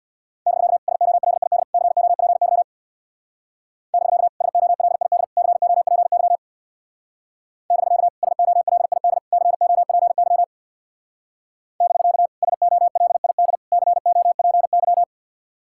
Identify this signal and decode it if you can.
Morse code, heard as a keyed tone, about 61 wpm, 700 Hz. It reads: <BK> SOLID COPY <BK> SOLID COPY <BK> SOLID COPY <BK> SOLID COPY
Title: Speed-Racing Common QSO Phrases - 50wpm